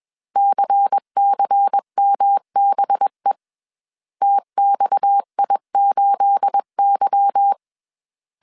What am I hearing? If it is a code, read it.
Morse code, heard as a keyed tone, about 21 wpm, 800 Hz. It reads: CCMBE TXI8Y
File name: CW / DET / 1 9 1500C S C 20wpm 2310